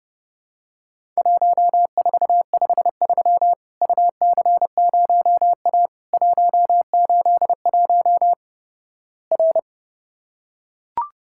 Morse code, heard as a keyed tone, about 31 wpm, 700 Hz. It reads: 1453 UC0A 181 R
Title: Worked All Europe DX Contest - QTC Exchange - 30wpm